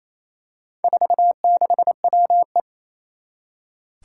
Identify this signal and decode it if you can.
Morse code, heard as a keyed tone, about 28 wpm, 700 Hz. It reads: 46WE
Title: Four Challenging Letter-Number Combinations 28wpm